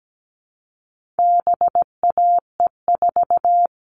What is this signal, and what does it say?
Morse code, heard as a keyed tone, about 17 wpm, 700 Hz. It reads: BAE4
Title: All Cut Numbers - 4-Digit 17wpm